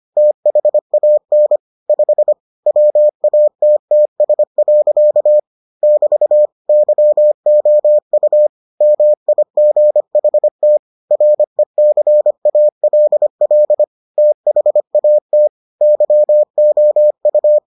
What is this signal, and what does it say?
Morse code, heard as a keyed tone, about 25 wpm, 600 Hz. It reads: THAN 5 WATTS. <BT> YOU MIGHT RECALL THAT YOU